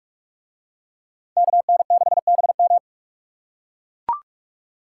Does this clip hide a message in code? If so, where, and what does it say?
Yes: Morse code from 1.4 to 2.8 s, heard as a keyed tone, about 43 wpm, 700 Hz. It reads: KN6BM